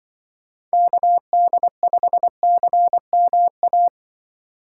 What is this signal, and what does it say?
Morse code, heard as a keyed tone, about 24 wpm, 700 Hz. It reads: KD5CMA
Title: Speed Racing - USA 2x3 Call Signs 20wpm